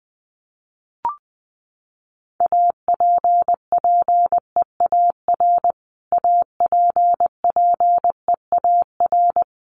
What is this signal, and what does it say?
Morse code, heard as a keyed tone, about 20 wpm, 700 Hz. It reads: APPEAR APPEAR
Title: Top 500 Words Repeated 20wpm